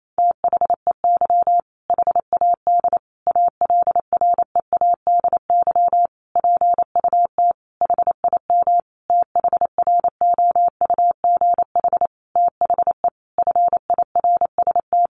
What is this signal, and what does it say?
Morse code, heard as a keyed tone, about 28 wpm, 700 Hz. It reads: THEY HAD ALREADY PUT HIM THROUGH THE FIRST
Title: Sentences from Top 1200 Words 28wpm